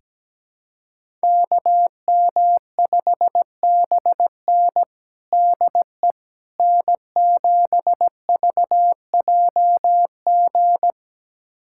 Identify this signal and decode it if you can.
Morse code, heard as a keyed tone, about 17 wpm, 700 Hz. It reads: KM5BN DE N7VJG
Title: USA Callsign DE Callsign - 5 Characters - 17wpm